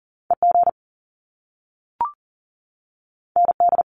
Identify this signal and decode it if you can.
Morse code, heard as a keyed tone, about 38 wpm, 700 Hz. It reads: EG ND
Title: Top Two Letter Combinations in Words 40wpm